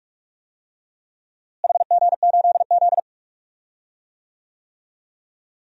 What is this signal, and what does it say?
Morse code, heard as a keyed tone, about 46 wpm, 700 Hz. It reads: HG8Z
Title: DXCC-100 Call Signs - 4 Characters - Repeated 3x - 45wpm